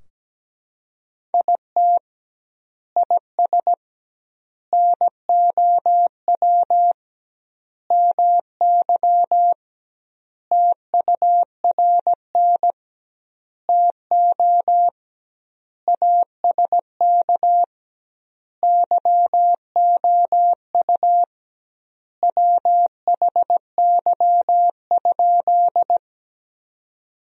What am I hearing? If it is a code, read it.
Morse code, heard as a keyed tone, about 17 wpm, 700 Hz. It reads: IT IS NOW MY TURN TO ASK YOU WHY?
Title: 2x Word-Spacing - Sentences from Top 200 Words - 17wpm